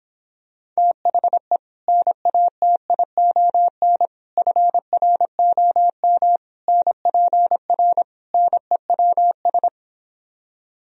Morse code, heard as a keyed tone, about 26 wpm, 700 Hz. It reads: THE NATION FROM NPR NEWS